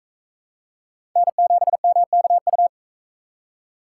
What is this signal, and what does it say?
Morse code, heard as a keyed tone, about 38 wpm, 700 Hz. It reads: N7MKU